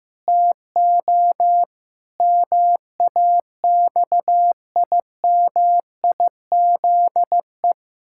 Morse code, heard as a keyed tone, about 16 wpm, 700 Hz. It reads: TO MAXIMIZE
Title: Pangram Challenge v2 - 15wpm